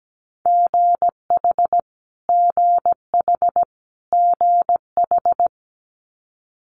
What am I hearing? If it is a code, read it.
Morse code, heard as a keyed tone, about 17 wpm, 700 Hz. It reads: GH GH GH
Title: Multi-letter Phonemes Repeated 3x 17wpm